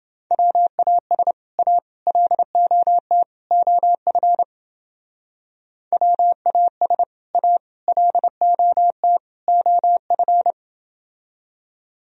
Morse code, heard as a keyed tone, about 28 wpm, 700 Hz. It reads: WAS A LOT OF WAS A LOT OF